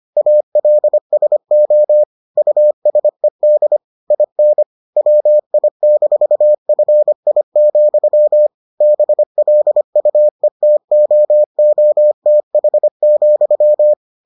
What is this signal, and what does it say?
Morse code, heard as a keyed tone, about 25 wpm, 600 Hz. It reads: ALSO USED IN WI-FI, BLUETOOTH,